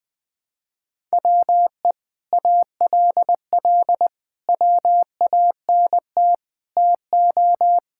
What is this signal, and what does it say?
Morse code, heard as a keyed tone, about 21 wpm, 700 Hz. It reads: WE ALL WANT TO
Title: Sentences with 7-Word Phrases - 20wpm